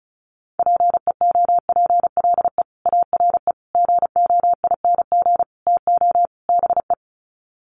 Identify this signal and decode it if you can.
Morse code, heard as a keyed tone, about 34 wpm, 700 Hz. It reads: PEOPLE ARE GOING TO BE
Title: Sets of 5 Words Encore 35wpm